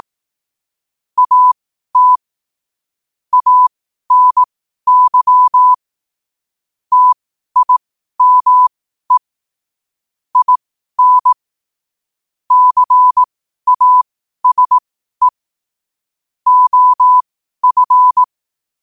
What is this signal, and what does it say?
Morse code, heard as a keyed tone, about 18 wpm, spaced as for 12 wpm, 1000 Hz. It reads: AT ANY TIME IN CASE OF